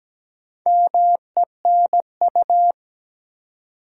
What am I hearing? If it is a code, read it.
Morse code, heard as a keyed tone, about 17 wpm, 700 Hz. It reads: MENU
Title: Four Letter Words - 17wpm